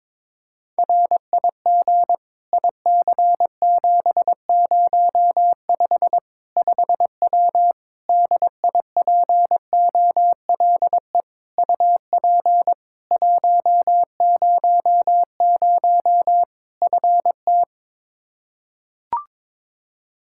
Morse code, heard as a keyed tone, about 22 wpm, 700 Hz. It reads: RIG IC705 5W DIPOLE UP 100 FT
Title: K1USN SST QSO Practice 22wpm